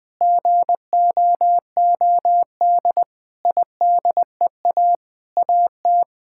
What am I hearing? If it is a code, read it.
Morse code, heard as a keyed tone, about 20 wpm, 700 Hz. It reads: GOOD IDEA AT
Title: Sets of 9 Words - 20wpm